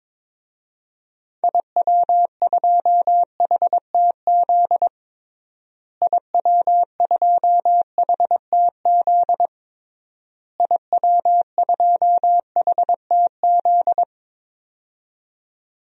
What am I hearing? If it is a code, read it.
Morse code, heard as a keyed tone, about 21 wpm, 700 Hz. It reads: IW2HTZ IW2HTZ IW2HTZ
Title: DXCC-100 Call Signs - 6 Characters - Repeated 3x - 22wpm